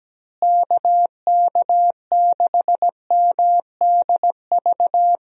Morse code, heard as a keyed tone, about 17 wpm, 700 Hz. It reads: KK6MDV
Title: USA Call Signs - 6 Characters - Repeated 3x - 17wpm